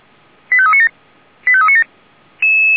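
Five-tone selective call (ZVEI), heard as quick runs of five tones, tones of about 80 ms. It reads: 85397 85397
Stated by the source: ZVEI five-tone selective call received on an SDR for Feuerwehr Orlamünde.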